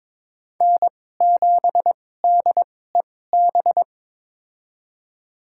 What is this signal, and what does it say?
Morse code, heard as a keyed tone, about 22 wpm, spaced as for 16 wpm, 700 Hz. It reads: N7DEB
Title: USA Call Signs - 5 Characters - 2x Character Spacing - 22wpm